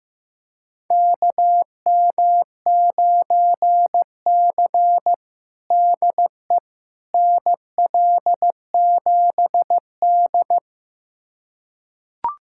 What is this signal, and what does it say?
Morse code, heard as a keyed tone, about 15 wpm, 700 Hz. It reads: KM9C DE NL7D
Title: USA Callsign DE Callsign - 4 Characters - 15wpm